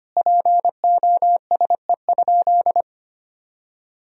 Morse code, heard as a keyed tone, about 25 wpm, 700 Hz. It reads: POSE?
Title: Sentences from Top 1000 Words - Review Part 2 - 25wpm